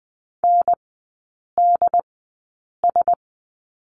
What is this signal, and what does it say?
Morse code, heard as a keyed tone, about 20 wpm, spaced as for 8 wpm, 690 Hz. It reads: NDS
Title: 081 Lesson 14e - Number 2 - Three Characters 20wpm